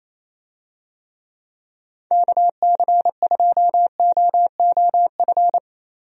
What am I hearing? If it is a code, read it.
Morse code, heard as a keyed tone, about 28 wpm, 700 Hz. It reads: KC2OOF
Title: USA Call Signs - 6 Characters - Repeated 3x - 28wpm